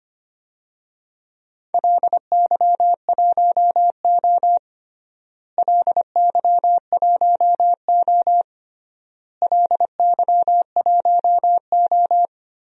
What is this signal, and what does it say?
Morse code, heard as a keyed tone, about 25 wpm, 700 Hz. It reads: LY1O LY1O LY1O
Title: DXCC-100 Call Signs - 4 Characters - Repeated 3x - 25wpm